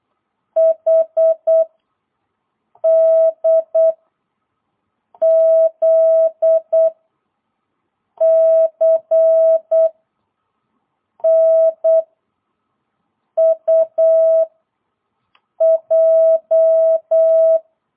Morse code, heard as a keyed tone, about 8 wpm, 660 Hz. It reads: H D Z C N U J
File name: jh8cbh / 2022cw